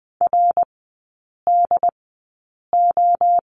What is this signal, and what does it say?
Morse code, heard as a keyed tone, about 20 wpm, spaced as for 8 wpm, 700 Hz. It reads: RDO
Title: 127 Lesson 21e - Number 6 - Three Characters 20wpm